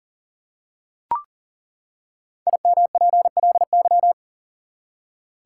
Morse code, heard as a keyed tone, about 40 wpm, 700 Hz. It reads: IMPLY
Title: Five Letter Words - 40wpm